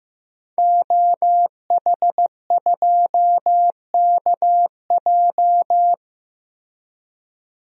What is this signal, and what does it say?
Morse code, heard as a keyed tone, about 15 wpm, 700 Hz. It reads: OH2KJ
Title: DXCC-100 Call Signs - 5 Characters - Repeated 2x - 15wpm